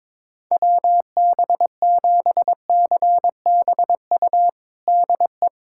Morse code, heard as a keyed tone, about 22 wpm, 700 Hz. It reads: WB7CBU DE
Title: USA Callsign DE Callsign - 6 Characters - 22wpm